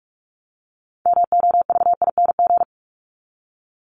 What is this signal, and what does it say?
Morse code, heard as a keyed tone, about 44 wpm, 700 Hz. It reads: MOVING
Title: ING Suffix - Top 100 Words with Base Words 45wpm